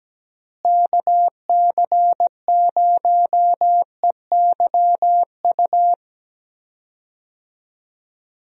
Morse code, heard as a keyed tone, about 17 wpm, 700 Hz. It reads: KC0EYU
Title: USA Call Signs - 6 Characters - Repeated 3x - 17wpm